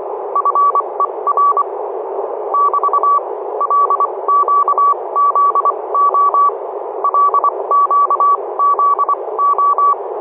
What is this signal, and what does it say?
Morse code, heard as a keyed tone, about 25 wpm, spaced as for 20 wpm, 1140 Hz, with noise. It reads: FER <BT> LQZO LQZO
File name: oxb-tfc-list